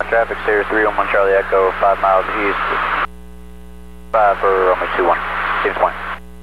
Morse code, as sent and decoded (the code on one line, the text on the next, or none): none